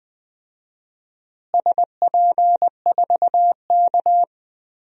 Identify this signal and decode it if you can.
Morse code, heard as a keyed tone, about 20 wpm, 700 Hz. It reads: SP4K
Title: DXCC-100 Call Signs - 4 Characters - Repeated 3x - 20wpm